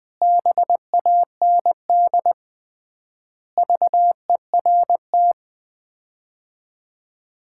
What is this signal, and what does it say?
Morse code, heard as a keyed tone, about 21 wpm, 700 Hz. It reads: BAND VERT
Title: Common QSO Phrases - 3x Word-Spacing - 20wpm